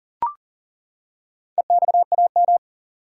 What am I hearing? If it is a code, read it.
Morse code, heard as a keyed tone, about 40 wpm, 700 Hz. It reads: EXAM
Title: Four Letter Words - 40wpm